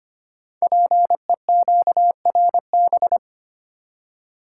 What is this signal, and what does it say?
Morse code, heard as a keyed tone, about 25 wpm, 700 Hz. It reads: PEQRB
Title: Code Groups - 5 Letters - 25wpm